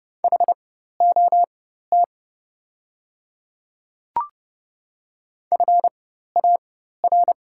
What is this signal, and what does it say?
Morse code, heard as a keyed tone, about 31 wpm, spaced as for 14 wpm, 700 Hz. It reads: HOT FAR
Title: 4x Character Spacing - Top 300 Words - 30wpm